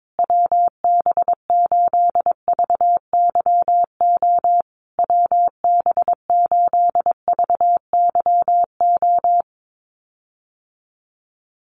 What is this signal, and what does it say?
Morse code, heard as a keyed tone, about 22 wpm, 700 Hz. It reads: WB8VYO WB8VYO
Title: USA Call Signs Repeated 3x 22wpm